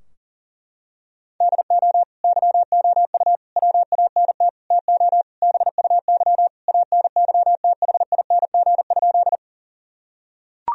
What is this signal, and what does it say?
Morse code, heard as a keyed tone, about 40 wpm, 700 Hz. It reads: DO YOU WANT TO BUY ANYTHING?
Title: Sentences from Top 1000 Words - Review Part 4 - 40wpm